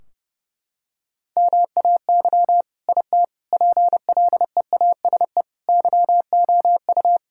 Morse code, heard as a keyed tone, about 30 wpm, 700 Hz. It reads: MAY IT PLEASE YOU
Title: Sentences from Top 1600 Words 30wpm